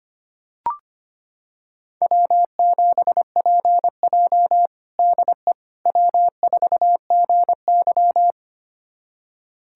Morse code, heard as a keyed tone, about 25 wpm, 700 Hz. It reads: W7PJ DE W4GY
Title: USA Callsign DE Callsign - 4 Characters - 25wpm